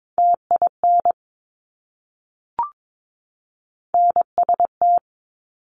Morse code, heard as a keyed tone, about 22 wpm, 700 Hz. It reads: TIN NST
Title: Top Three Letter Combinations in Words 22wpm